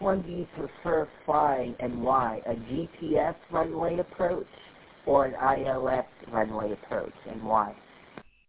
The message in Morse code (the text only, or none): none